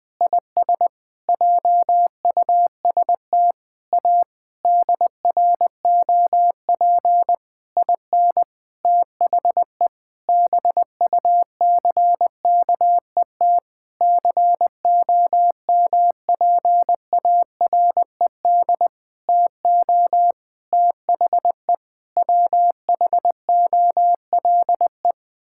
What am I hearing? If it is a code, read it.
Morse code, heard as a keyed tone, about 20 wpm, 700 Hz. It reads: IS JUST A DROP IN THE BUCKET COMPARED TO THE WHOLE
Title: Sentences with 7-Word Phrases - Spoken Beginning - 20wpm